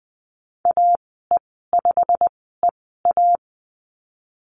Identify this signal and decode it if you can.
Morse code, heard as a keyed tone, about 20 wpm, spaced as for 15 wpm, 700 Hz. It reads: AE5EA
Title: 200 Lesson 32g - Letter k - Call Signs 20wpm